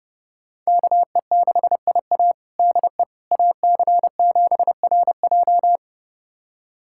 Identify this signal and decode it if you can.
Morse code, heard as a keyed tone, about 29 wpm, 700 Hz. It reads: KE6IA DE AC7RJ
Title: USA Callsign DE Callsign - 5 Characters - 30wpm